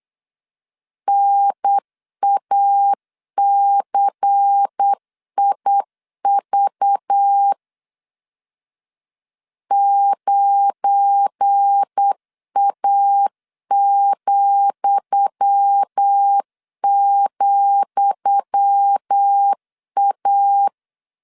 Morse code, heard as a keyed tone, about 8 wpm, 800 Hz. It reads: NACIV 9A,,A
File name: LEZIONI / 10 SIGNS / 10D COMA 8wpm 3749